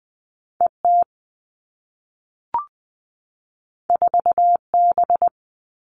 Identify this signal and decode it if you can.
Morse code, heard as a keyed tone, about 20 wpm, 700 Hz. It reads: ET 4B
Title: All Cut Numbers - 2-Digit 20wpm